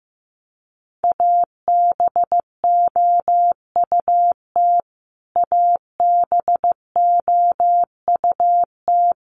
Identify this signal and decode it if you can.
Morse code, heard as a keyed tone, about 15 wpm, 700 Hz. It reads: ABOUT ABOUT
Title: Top 300 Words Repeated 15wpm